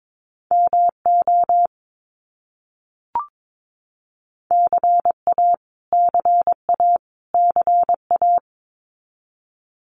Morse code, heard as a keyed tone, about 22 wpm, 700 Hz. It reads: MO CA CA CA
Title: US States Repeated 3x 22wpm